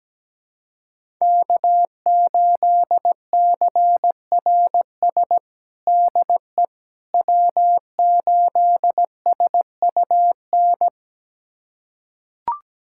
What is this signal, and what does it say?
Morse code, heard as a keyed tone, about 17 wpm, 700 Hz. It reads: K8CRS DE W8SUN
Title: USA Callsign DE Callsign - 5 Characters - 17wpm